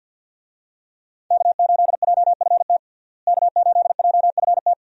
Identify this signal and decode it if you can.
Morse code, heard as a keyed tone, about 47 wpm, 700 Hz. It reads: K8JFT K8JFT
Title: USA Call Signs - 5 Characters - Repeated 2x - 50wpm